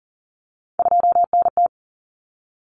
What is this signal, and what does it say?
Morse code, heard as a keyed tone, about 40 wpm, 700 Hz. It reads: 2NT
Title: Common Cut Numbers - 3-Digit 40wpm